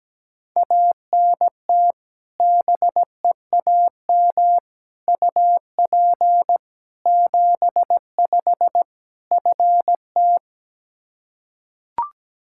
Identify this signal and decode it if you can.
Morse code, heard as a keyed tone, about 17 wpm, 700 Hz. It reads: ANT BEAM UP 75 FT
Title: Practice QSO - Exchange 2 - Info - 17wpm